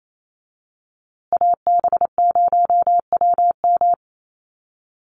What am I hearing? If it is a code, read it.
Morse code, heard as a keyed tone, about 28 wpm, 700 Hz. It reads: AB0WM
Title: USA Call Signs Repeated 2x 28wpm